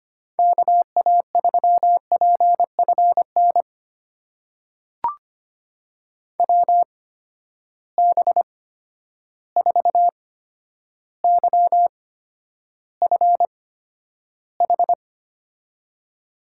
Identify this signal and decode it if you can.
Morse code, heard as a keyed tone, about 25 wpm, spaced as for 11 wpm, 700 Hz. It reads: KA3PFN WB4YFH
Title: USA Call Signs - 6 Characters - 8x Character Spacing - 25wpm